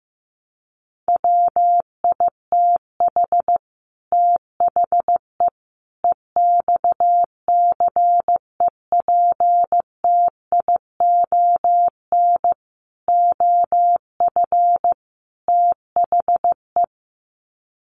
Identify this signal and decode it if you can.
Morse code, heard as a keyed tone, about 15 wpm, 700 Hz. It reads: WITH THE EXCEPTION OF THE
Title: Sets of 5 Words Encore 15wpm